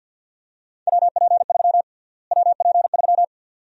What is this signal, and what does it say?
Morse code, heard as a keyed tone, about 53 wpm, 700 Hz. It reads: WP3 WP3
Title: USA Call Sign Prefixes - Repeated 2x - 50wpm